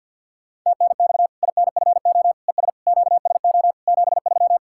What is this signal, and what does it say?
Morse code, heard as a keyed tone, about 48 wpm, 700 Hz. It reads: TNX INFO ES QSO 73
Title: Practice QSO - Exchange 4 - Ending - 50wpm